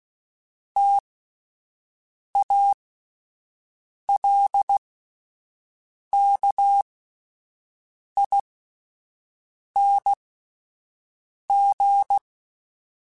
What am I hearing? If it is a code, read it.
Morse code, heard as a keyed tone, about 16 wpm, spaced as for 5 wpm, 800 Hz. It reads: TALKING